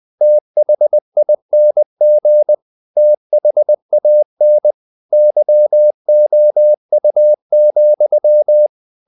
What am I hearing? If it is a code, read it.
Morse code, heard as a keyed tone, about 20 wpm, 600 Hz. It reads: THING THAN YOU,
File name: the old man and the sea part 2 20wpm